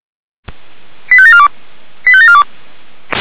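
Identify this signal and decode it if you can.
Five-tone selective call (ZVEI), heard as quick runs of five tones, tones of about 70 ms. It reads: 85742 85742
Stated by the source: ZVEI five-tone selective call received on SDR.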